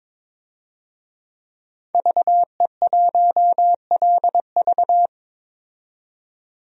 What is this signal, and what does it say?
Morse code, heard as a keyed tone, about 22 wpm, 700 Hz. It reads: VE1LV
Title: DXCC-100 Call Signs - 5 Characters - 22wpm